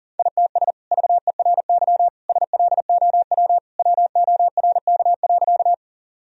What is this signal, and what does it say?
Morse code, heard as a keyed tone, about 37 wpm, 700 Hz. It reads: ITS VERY SLOW WORK.
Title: Sentences from Top 1000 Words - Review Part 2 - 40wpm